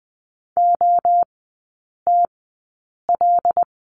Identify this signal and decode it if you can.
Morse code, heard as a keyed tone, about 20 wpm, spaced as for 8 wpm, 700 Hz. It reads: OTL
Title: 074 Lesson 13e - Letter L - Three Characters 20wpm